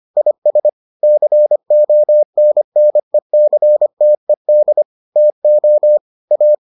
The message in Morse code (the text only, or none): IS CONNECTED TO A